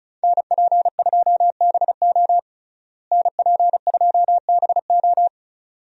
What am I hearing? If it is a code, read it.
Morse code, heard as a keyed tone, about 36 wpm, 700 Hz. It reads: NP2BO NP2BO
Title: USA Call Signs - 5 Characters - Repeated 2x - 35wpm